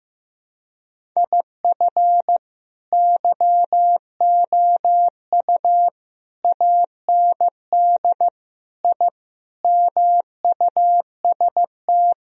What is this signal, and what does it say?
Morse code, heard as a keyed tone, about 15 wpm, 700 Hz. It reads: IF YOU AND I MUST
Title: Sentences from Top 1000 Words - Review Part 1 - 15wpm